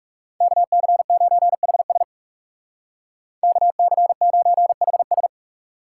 Morse code, heard as a keyed tone, about 37 wpm, 700 Hz. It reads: KC9HS KC9HS
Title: Speed Racing - USA 2x2 Call Signs 30wpm